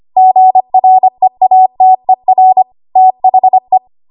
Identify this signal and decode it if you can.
Morse code, heard as a keyed tone, about 25 wpm, 750 Hz. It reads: GREATER THE
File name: Morse / 25WPM